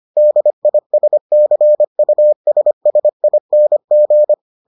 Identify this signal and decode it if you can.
Morse code, heard as a keyed tone, about 25 wpm, 600 Hz. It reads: DISCUSSING